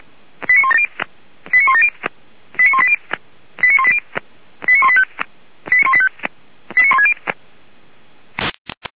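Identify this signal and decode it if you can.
Five-tone selective call (ZVEI), heard as quick runs of five tones, tones of about 70 ms. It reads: 89170 89170 89180 89180 89185 89160
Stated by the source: ZVEI five-tone call received by SDR, recorded by feuerwehr-orlamuende.org fire brigade.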